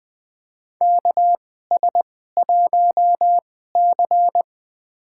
Morse code, heard as a keyed tone, about 20 wpm, spaced as for 15 wpm, 700 Hz. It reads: KS1C
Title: USA Call Signs - 4 Characters - 2x Character Spacing - 20wpm